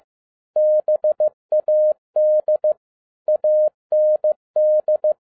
Morse code, heard as a keyed tone, about 15 wpm, 610 Hz. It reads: BAD AND
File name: 2011-07-mp3 / n57 BACK FROM A BIG ACCIDENT 15wpm